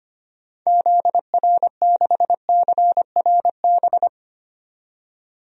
Morse code, heard as a keyed tone, about 25 wpm, 700 Hz. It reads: ZR6CRB
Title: DXCC-100 Call Signs - 6 Characters - 25wpm